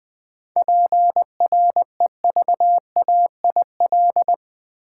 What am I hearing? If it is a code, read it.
Morse code, heard as a keyed tone, about 20 wpm, 700 Hz. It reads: PREVAIL
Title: 7 Letter Words - First 3 Letters Given - 20wpm